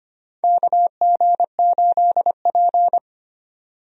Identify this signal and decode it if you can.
Morse code, heard as a keyed tone, about 25 wpm, 700 Hz. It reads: KG8P
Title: USA Call Signs - 4 Characters - 25wpm